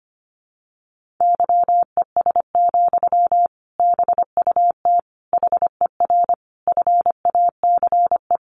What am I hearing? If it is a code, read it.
Morse code, heard as a keyed tone, about 25 wpm, 700 Hz. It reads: YES, BUT HER FACE
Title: Sentences from Top 1200 Words 25wpm